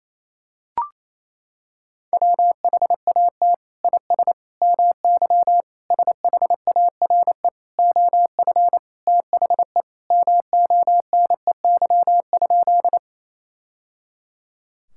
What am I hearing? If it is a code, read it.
Morse code, heard as a keyed tone, about 27 wpm, 700 Hz. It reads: WHAT IS MY SHARE OF THE MONEY?
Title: Sentences from Top 1000 Words - Review Part 6 - 28wpm